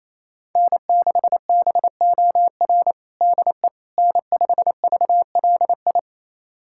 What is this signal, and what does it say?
Morse code, heard as a keyed tone, about 28 wpm, 700 Hz. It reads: N6BOR DE N5VLI
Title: USA Callsign DE Callsign - 5 Characters - 28wpm